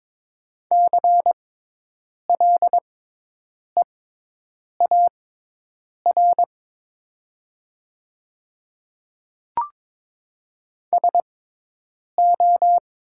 Morse code, heard as a keyed tone, about 22 wpm, spaced as for 8 wpm, 700 Hz. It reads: CLEAR SO
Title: 6x Character Spacing - Top 500 Words - 22wpm